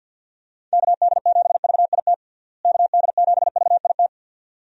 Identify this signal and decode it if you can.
Morse code, heard as a keyed tone, about 52 wpm, 700 Hz. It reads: KD7VIT KD7VIT
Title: USA Call Signs - 6 Characters - Repeated 2x - 50wpm